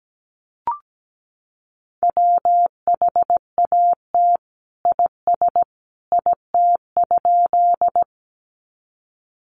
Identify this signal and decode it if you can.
Morse code, heard as a keyed tone, about 17 wpm, 700 Hz. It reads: WHAT IS IT?
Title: Sentences from Top 100 Words 17wpm